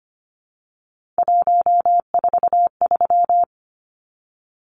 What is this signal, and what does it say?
Morse code, heard as a keyed tone, about 25 wpm, 700 Hz. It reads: 143
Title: Numbers - 3-Digit 25wpm